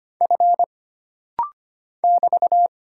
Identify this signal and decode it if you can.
Morse code, heard as a keyed tone, about 25 wpm, 700 Hz. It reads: F <BT>
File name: Single Character (Letter:Number:Punctuation:Prosign) - Rapid-Fire - 25wpm